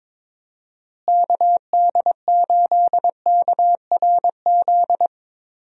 Morse code, heard as a keyed tone, about 22 wpm, 700 Hz. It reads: KD8KRZ
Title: USA Call Signs - 6 Characters - Repeated 2x - 22wpm